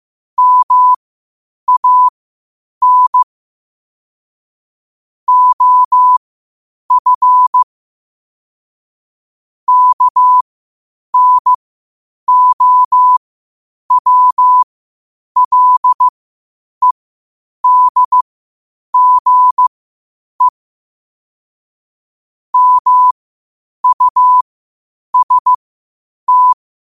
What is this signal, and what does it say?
Morse code, heard as a keyed tone, about 15 wpm, spaced as for 8 wpm, 1000 Hz. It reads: MAN OF KNOWLEDGE MUST
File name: qotdfiles / 8wpm